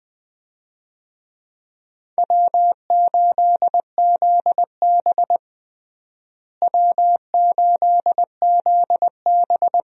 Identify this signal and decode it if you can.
Morse code, heard as a keyed tone, about 21 wpm, 700 Hz. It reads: W8ZB W8ZB
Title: USA Call Signs - 4 Characters - Repeated 3x - 20wpm